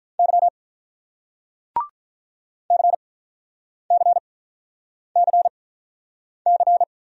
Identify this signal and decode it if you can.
Morse code, heard as a keyed tone, about 46 wpm, spaced as for 9 wpm, 700 Hz. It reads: Y CCCC